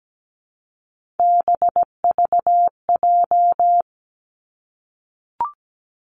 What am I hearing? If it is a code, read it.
Morse code, heard as a keyed tone, about 17 wpm, 700 Hz. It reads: BVJ
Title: Three Challenging Letter-Number Combinations 17wpm